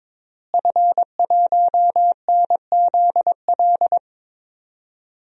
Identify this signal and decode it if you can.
Morse code, heard as a keyed tone, about 22 wpm, 700 Hz. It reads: F1NZL
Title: DXCC-100 Call Signs - 5 Characters - Repeated 3x - 22wpm